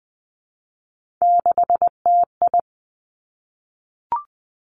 Morse code, heard as a keyed tone, about 20 wpm, 700 Hz. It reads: 6TI
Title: Three Challenging Letter-Number Combinations 20wpm